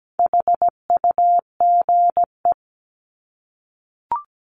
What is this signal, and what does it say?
Morse code, heard as a keyed tone, about 17 wpm, 700 Hz. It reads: HUGE
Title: Top 1001-1250 Words 17wpm